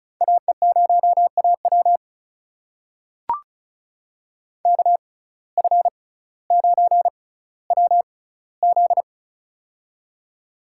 Morse code, heard as a keyed tone, about 36 wpm, spaced as for 18 wpm, 700 Hz. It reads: AE0AW KF9WZ
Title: USA Call Signs - 5 Characters - 6x Character Spacing - 35wpm